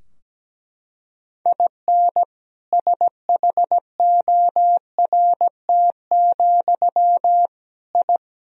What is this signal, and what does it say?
Morse code, heard as a keyed tone, about 17 wpm, 700 Hz. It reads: IN SHORT, I
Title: Sentences from Top 1000 Words - Review Part 2 - 17wpm